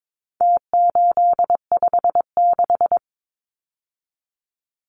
Morse code, heard as a keyed tone, about 22 wpm, 700 Hz. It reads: T856
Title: Numbers - Contest Style - 4-Digit 22wpm